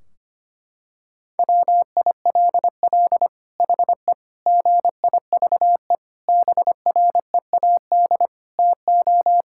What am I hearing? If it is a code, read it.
Morse code, heard as a keyed tone, about 25 wpm, 700 Hz. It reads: WILL HE GIVE BREAD TO
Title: Sentences from Top 1600 Words 25wpm